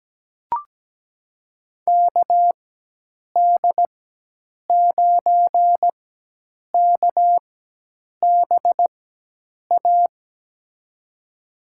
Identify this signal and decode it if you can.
Morse code, heard as a keyed tone, about 17 wpm, spaced as for 8 wpm, 700 Hz. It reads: KD9KBA
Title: USA Call Signs - 4x Character Spacing - 17wpm